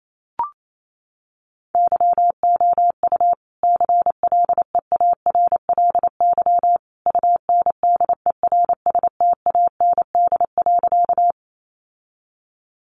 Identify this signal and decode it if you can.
Morse code, heard as a keyed tone, about 28 wpm, 700 Hz. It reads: YOU CLEARLY UNDERSTAND.
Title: Sentences from Top 1300 Words 28wpm